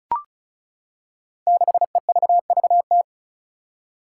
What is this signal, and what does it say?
Morse code, heard as a keyed tone, about 36 wpm, 700 Hz. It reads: 6EVVT
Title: Five Challenging Letter-Number Combinations 35wpm